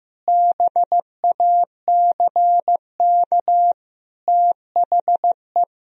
Morse code, heard as a keyed tone, about 15 wpm, 700 Hz. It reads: BACK THE
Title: Sentences from Top 1000 Words - Review Part 4 - 15wpm